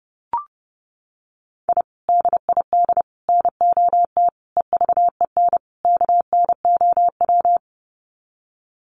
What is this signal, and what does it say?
Morse code, heard as a keyed tone, about 30 wpm, 700 Hz. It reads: I DID NOT EVEN KNOW
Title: Sets of 5 Words Encore 30wpm